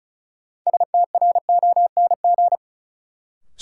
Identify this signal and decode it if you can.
Morse code, heard as a keyed tone, about 36 wpm, 700 Hz. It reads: STRONG
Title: Sentences from Top 500 Words - New Word Drill - 35wpm